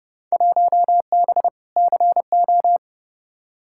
Morse code, heard as a keyed tone, about 30 wpm, 700 Hz. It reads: 1B CO
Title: Field Day Exchanges - 30wpm